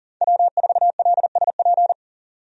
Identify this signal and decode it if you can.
Morse code, heard as a keyed tone, about 40 wpm, 700 Hz. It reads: W4LSP
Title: USA Call Signs - 5 Characters - 40wpm